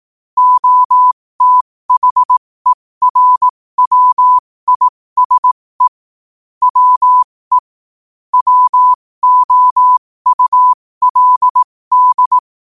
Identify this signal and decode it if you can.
Morse code, heard as a keyed tone, about 18 wpm, spaced as for 15 wpm, 1000 Hz. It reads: OTHERWISE WE WOULD
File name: qotdfiles / 15wpm